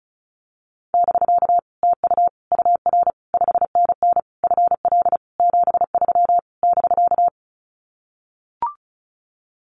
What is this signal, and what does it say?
Morse code, heard as a keyed tone, about 34 wpm, 700 Hz. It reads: <BK> TU UR 5NN FL 73 <BK>
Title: Common QSO Phrases Encore 35wpm